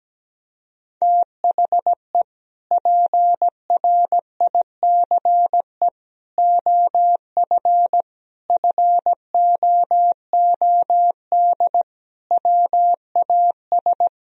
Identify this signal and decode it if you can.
Morse code, heard as a keyed tone, about 17 wpm, 700 Hz. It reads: THE PRICE OF FOOD WAS
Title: Sentences from Top 1000 Words - Review Part 4 - 17wpm